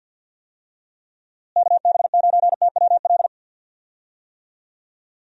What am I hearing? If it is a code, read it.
Morse code, heard as a keyed tone, about 48 wpm, 700 Hz. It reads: KB9TWL
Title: USA Call Signs - 6 Characters - Repeated 2x - 50wpm